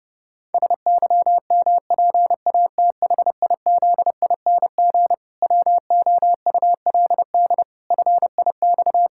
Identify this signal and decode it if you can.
Morse code, heard as a keyed tone, about 30 wpm, 700 Hz. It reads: SYMPATHIZING WOULD FIX
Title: Pangram Challenge v2 - 30wpm